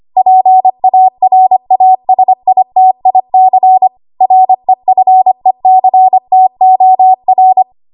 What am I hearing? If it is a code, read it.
Morse code, heard as a keyed tone, about 25 wpm, 750 Hz. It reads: PARASITIC REFECTOR
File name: Archive / 25 WPM / 25WPM